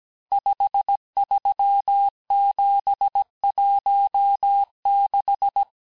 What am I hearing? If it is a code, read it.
Morse code, heard as a keyed tone, about 17 wpm, 800 Hz. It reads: 53716
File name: corrige-cours-1581-texte-chiffre-vitesse-16-mpm